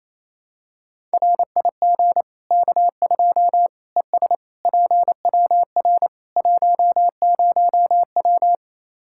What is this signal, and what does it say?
Morse code, heard as a keyed tone, about 28 wpm, 700 Hz. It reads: RIG K2 ES PWR 10W
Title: Practice QSO - Exchange 2 - Info - 28wpm